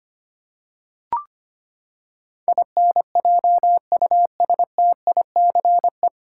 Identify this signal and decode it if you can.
Morse code, heard as a keyed tone, about 25 wpm, 700 Hz. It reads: INJUSTICE
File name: Nine Letter Words - 25wpm